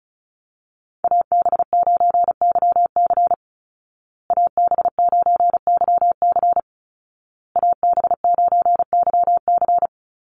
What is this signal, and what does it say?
Morse code, heard as a keyed tone, about 34 wpm, 700 Hz. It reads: AB9YC AB9YC AB9YC
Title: Call Signs Repeated 2x - 35wpm